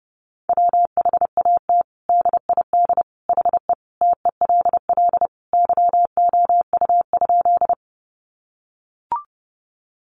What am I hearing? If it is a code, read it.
Morse code, heard as a keyed tone, about 30 wpm, 700 Hz. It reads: WHAT DID HE TELL YOU?
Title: Sentences from Top 200 Words 30wpm